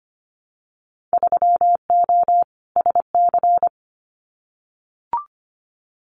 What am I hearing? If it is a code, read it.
Morse code, heard as a keyed tone, about 25 wpm, 700 Hz. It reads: 3O SC
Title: Winter Field Day Exchanges Encore 25wpm